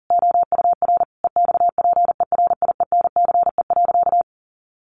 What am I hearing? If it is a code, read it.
Morse code, heard as a keyed tone, about 40 wpm, 700 Hz. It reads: OUR EXPERIENCE.
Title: Sentences from Top 1200 Words 40wpm